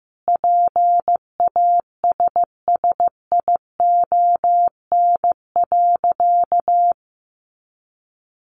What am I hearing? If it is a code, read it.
Morse code, heard as a keyed tone, about 15 wpm, 700 Hz. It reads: PASSION.
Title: Sentences from Top 1300 Words 15wpm